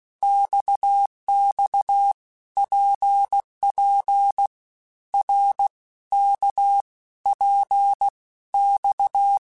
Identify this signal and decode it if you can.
Morse code, heard as a keyed tone, about 16 wpm, 800 Hz. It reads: XX PP R K P X
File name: Basic Letter Groups By7Groups / RKPX / RKPX@15